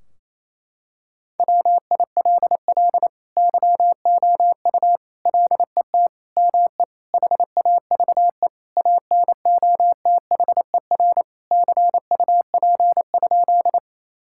Morse code, heard as a keyed tone, about 28 wpm, 700 Hz. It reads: WILL YOU LET ME HAVE ANOTHER CUP?
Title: Sentences from Top 1600 Words 28wpm